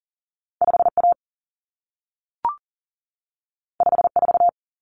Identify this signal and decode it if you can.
Morse code, heard as a keyed tone, about 40 wpm, 700 Hz. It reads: 5A 54
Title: Common Cut Numbers - 2-Digit 40wpm